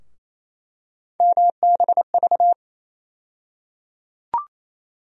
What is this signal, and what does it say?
Morse code, heard as a keyed tone, about 27 wpm, 700 Hz. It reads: MBV
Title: USA Call Sign Postfixes - 3 Letters 28wpm